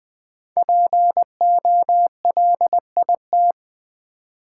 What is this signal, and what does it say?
Morse code, heard as a keyed tone, about 20 wpm, 700 Hz. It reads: POLIT
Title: Build A Word - 5 Letters At A Time - 10-Letter Words - 20wpm